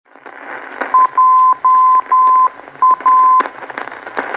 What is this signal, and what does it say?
Morse code, heard as a keyed tone, about 10 wpm, 1030 Hz, with radio noise. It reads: JA
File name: beacon program / tracks / 01 JA Beacon